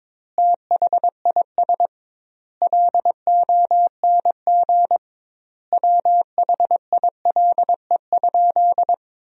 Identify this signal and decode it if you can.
Morse code, heard as a keyed tone, about 22 wpm, 700 Hz. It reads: THIS LONG WHILE?
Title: 2x Word-Spacing - Sentences from Top 300 Words - 22wpm